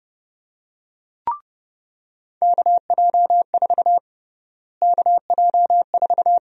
Morse code, heard as a keyed tone, about 30 wpm, 700 Hz. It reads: KJ4 KJ4
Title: USA Call Sign Prefixes - Repeated 2x - 30wpm